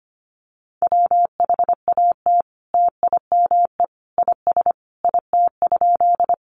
Morse code, heard as a keyed tone, about 25 wpm, 700 Hz. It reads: WHAT TIME IS IT?
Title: Sentences from Top 100 Words 25wpm